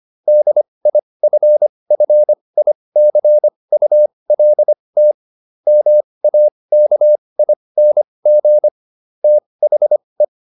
Morse code, heard as a keyed tone, about 25 wpm, spaced as for 20 wpm, 600 Hz. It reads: DIFFICULT MAKING THE
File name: SS316 20